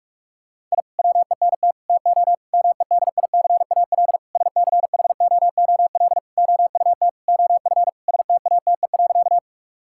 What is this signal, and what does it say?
Morse code, heard as a keyed tone, about 47 wpm, 700 Hz. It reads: I WENT TO MEDICAL SCHOOL OUT OF STATE.
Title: Sentences from Top 1000 Words - Review Part 5 - 45wpm